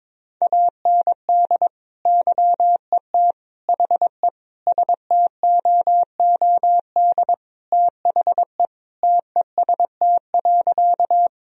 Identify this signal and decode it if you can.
Morse code, heard as a keyed tone, about 22 wpm, 700 Hz. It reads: AND YET HE STOOD THE TEST.
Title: Sentences from Top 1000 Words - Review Part 3 - 22wpm